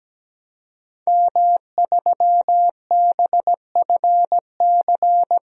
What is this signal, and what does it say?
Morse code, heard as a keyed tone, about 17 wpm, 700 Hz. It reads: M3BFC
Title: DXCC-100 Call Signs - 5 Characters - 17wpm